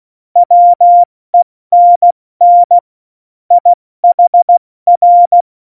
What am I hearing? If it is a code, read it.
Morse code, heard as a keyed tone, about 16 wpm, 700 Hz. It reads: WENN IHR
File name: Morsekurs / Witze / Witz 10.Dezember 2019, 70 F80